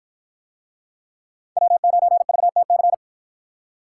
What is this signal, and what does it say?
Morse code, heard as a keyed tone, about 52 wpm, 700 Hz. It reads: W9VTC